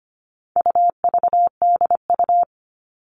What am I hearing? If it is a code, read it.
Morse code, heard as a keyed tone, about 25 wpm, 700 Hz. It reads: UVDU
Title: All Cut Numbers - 4-Digit 25wpm